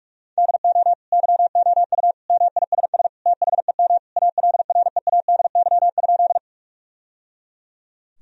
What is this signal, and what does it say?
Morse code, heard as a keyed tone, about 46 wpm, 700 Hz. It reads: DO YOU MISS THEM ALREADY?
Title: Sentences from Top 1000 Words - Review Part 3 - 45wpm